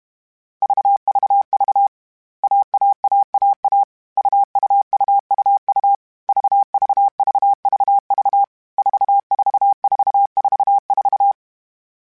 Morse code, heard as a keyed tone, about 32 wpm, 800 Hz. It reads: VVV AAAAA UUUUU VVVVV 44444